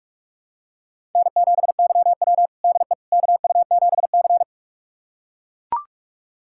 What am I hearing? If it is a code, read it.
Morse code, heard as a keyed tone, about 43 wpm, 700 Hz. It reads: N7YW DE KU7C
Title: USA Callsign DE Callsign - 4 Characters - 45wpm